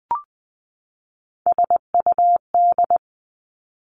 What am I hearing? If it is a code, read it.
Morse code, heard as a keyed tone, about 19 wpm, 700 Hz. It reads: SUD
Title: Three Challenging Letter-Number Combinations 20wpm